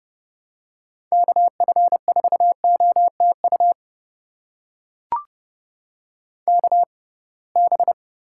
Morse code, heard as a keyed tone, about 30 wpm, spaced as for 23 wpm, 700 Hz. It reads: KF4OTU KB
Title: USA Call Signs - 6 Characters - 6x Character Spacing - 30wpm